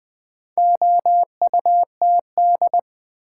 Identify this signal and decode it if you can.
Morse code, heard as a keyed tone, about 20 wpm, 700 Hz. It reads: OUTD
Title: Build A Word - 4 Letters At A Time - 8-Letter Words - 20wpm